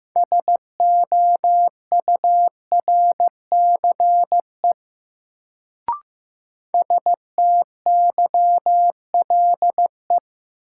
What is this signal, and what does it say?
Morse code, heard as a keyed tone, about 15 wpm, 700 Hz. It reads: SOURCE STYLE
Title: Sentences from Top 800 Words - New Word Drill - 15wpm